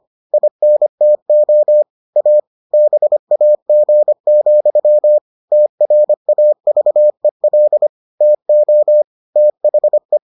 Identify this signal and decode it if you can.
Morse code, heard as a keyed tone, about 25 wpm, 600 Hz. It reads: INTO A BAG, TRAVEL TO THE